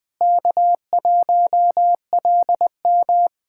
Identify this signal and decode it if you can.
Morse code, heard as a keyed tone, about 21 wpm, 700 Hz. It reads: K1LM
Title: USA Callsign DE Callsign - 4 Characters - 20wpm